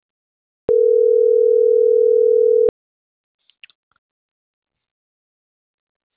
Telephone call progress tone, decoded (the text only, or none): ringback tone